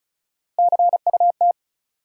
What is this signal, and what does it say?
Morse code, heard as a keyed tone, about 35 wpm, 700 Hz. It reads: CUT
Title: Sentences from Top 300 Words - New Word Drill - 35wpm